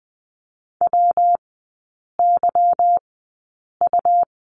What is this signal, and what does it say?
Morse code, heard as a keyed tone, about 20 wpm, spaced as for 8 wpm, 700 Hz. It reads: WYU
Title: 184 Lesson 30e - Letter b - Three Characters 20wpm